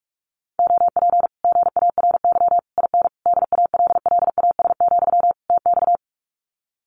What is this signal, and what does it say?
Morse code, heard as a keyed tone, about 46 wpm, 700 Hz. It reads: OP GARY IN DALLAS, TX
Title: Common QSO Phrases Encore 45wpm